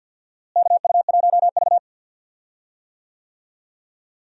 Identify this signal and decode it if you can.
Morse code, heard as a keyed tone, about 51 wpm, 700 Hz. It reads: KU1V
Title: USA Call Signs - 4 Characters - Repeated 3x - 50wpm